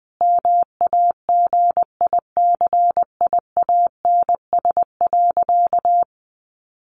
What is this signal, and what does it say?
Morse code, heard as a keyed tone, about 20 wpm, 700 Hz. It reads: MAGICIANS.
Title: Easy Phrases 20wpm